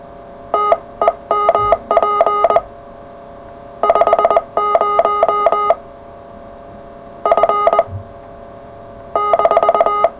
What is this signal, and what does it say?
Morse code, heard as a keyed tone, about 20 wpm, 1240 Hz, with noise. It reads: TEMP 50 F -
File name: full / cw qso